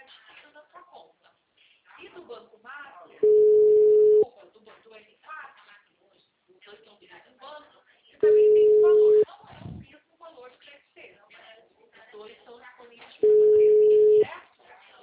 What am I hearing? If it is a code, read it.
Ringback tone (Europe, 425 Hz), with speech.